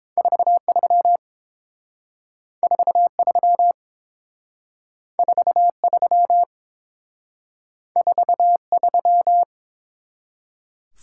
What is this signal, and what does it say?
Morse code, heard as a keyed tone, about 28 wpm, 700 Hz. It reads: 43 43 43 43